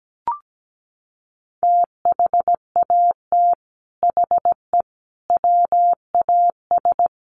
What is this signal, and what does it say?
Morse code, heard as a keyed tone, about 17 wpm, 700 Hz. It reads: THAT HE WAS
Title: Sets of 5 Words 17wpm